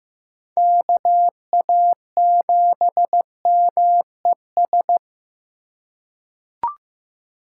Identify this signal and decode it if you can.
Morse code, heard as a keyed tone, about 15 wpm, 700 Hz. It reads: KA7MES
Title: USA Call Signs - 6 Characters - 6x Character Spacing - 15wpm